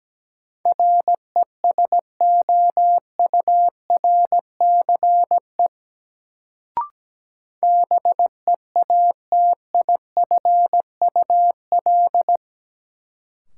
Morse code, heard as a keyed tone, about 17 wpm, 700 Hz. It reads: RESOURCE BEATIFUL
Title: Sentences from Top 1300 Words - New Word Drill - 17wpm